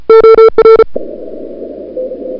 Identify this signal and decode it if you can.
Morse code, heard as a keyed tone, about 36 wpm, 440 Hz. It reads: OR
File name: CWSprint / February 2007 CW Sprint / 01-2007 CW Sprint Hour1